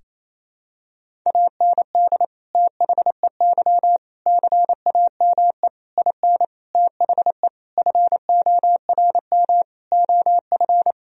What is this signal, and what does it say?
Morse code, heard as a keyed tone, about 28 wpm, 700 Hz. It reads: AND THEY CAME IN THE FORM OF
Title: Sentences from Top 1000 Words - Review Part 1 - 28wpm